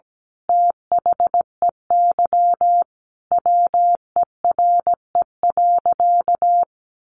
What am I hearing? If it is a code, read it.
Morse code, heard as a keyed tone, about 17 wpm, 700 Hz. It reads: THEY WERE.
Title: Sentences from Top 1000 Words 17wpm